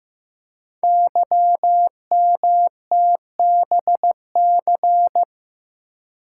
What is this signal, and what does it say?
Morse code, heard as a keyed tone, about 15 wpm, 700 Hz. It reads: YMTBC
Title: Code Groups - 5 Letters - 15wpm